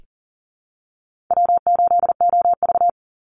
Morse code, heard as a keyed tone, about 40 wpm, 700 Hz. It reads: W8OV
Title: Easy Call Signs Repeated 3x - 40wpm